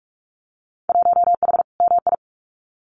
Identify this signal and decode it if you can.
Morse code, heard as a keyed tone, about 46 wpm, 700 Hz. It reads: 1H MI
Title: Winter Field Day Exchanges 45wpm